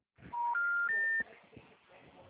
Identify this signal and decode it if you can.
Special information tone.
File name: out- -2009- - - .37545